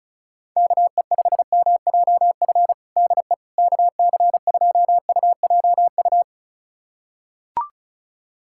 Morse code, heard as a keyed tone, about 34 wpm, 700 Hz. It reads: KE5MJF DE KC2UJU
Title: USA Callsign DE Callsign - 6 Characters - 35wpm